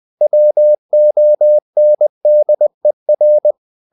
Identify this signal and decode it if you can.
Morse code, heard as a keyed tone, about 20 wpm, 600 Hz. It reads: WONDER
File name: the old man and the sea / the old man and the sea part 1 20wpm